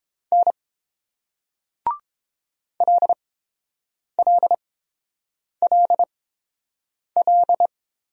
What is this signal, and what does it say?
Morse code, heard as a keyed tone, about 29 wpm, spaced as for 7 wpm, 690 Hz. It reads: N LLLL